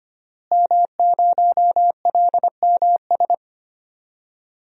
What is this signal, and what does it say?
Morse code, heard as a keyed tone, about 25 wpm, 700 Hz. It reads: M0LMS
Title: DXCC-100 Call Signs - 5 Characters - Repeated 2x - 25wpm